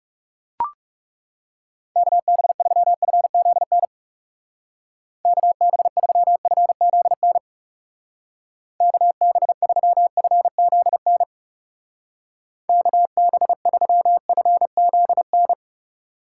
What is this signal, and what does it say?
Morse code, heard as a keyed tone, about 37 wpm, 700 Hz. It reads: KB3FZN KB3FZN KB3FZN KB3FZN
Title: Speed Racing - USA 2x3 Call Signs 30wpm